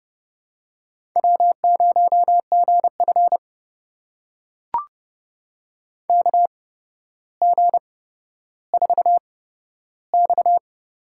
Morse code, heard as a keyed tone, about 30 wpm, spaced as for 13 wpm, 700 Hz. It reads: W0GF KG4X
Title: USA Call Signs - 4 Characters - 8x Character Spacing - 30wpm